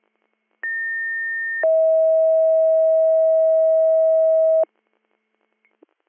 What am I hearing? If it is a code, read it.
Two-tone page, heard as a one-second tone, then a three-second tone, about 1800 then 640 Hz. It is heard from 0.6 to 4.6 s.